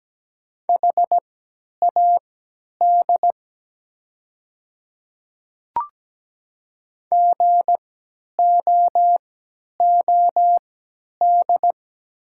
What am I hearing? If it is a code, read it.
Morse code, heard as a keyed tone, about 17 wpm, spaced as for 10 wpm, 700 Hz. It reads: HAD GOOD
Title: 3x Character Spacing - Top 200 Words - 17wpm